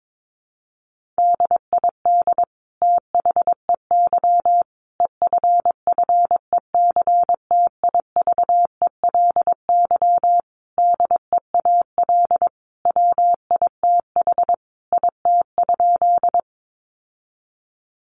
Morse code, heard as a keyed tone, about 22 wpm, 700 Hz. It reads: DID THEY EFFECTIVELY DEAL WITH IT?
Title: Sentences from Top 1400 Words 22wpm